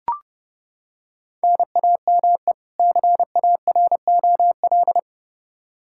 Morse code, heard as a keyed tone, about 30 wpm, 700 Hz. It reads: NAME CAROL
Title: Practice QSO - Exchange 1 - Beginning - 30wpm